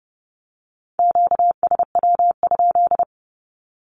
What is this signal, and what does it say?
Morse code, heard as a keyed tone, about 30 wpm, 700 Hz. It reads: QSW?
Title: Q-Codes with Spoken Meaning - 30wpm